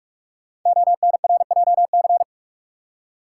Morse code, heard as a keyed tone, about 44 wpm, 690 Hz. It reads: ONRJC
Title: Code Groups - 5 Letters - 45wpm